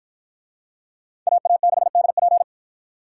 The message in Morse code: AA6DP

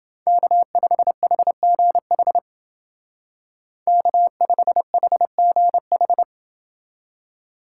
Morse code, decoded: K5HGH K5HGH

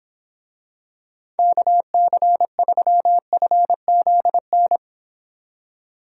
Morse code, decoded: KC3FZN

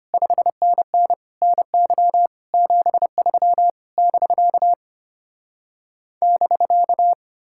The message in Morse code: 5NN NY 73 <BK> <BK>

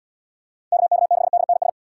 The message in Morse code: KK6DNS

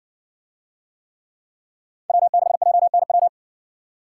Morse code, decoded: W6JNW

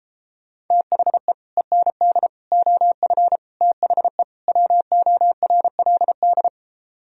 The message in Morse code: THE END OF THE WORLD

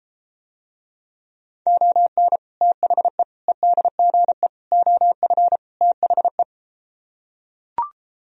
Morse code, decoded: ON THE EDGE OF THE

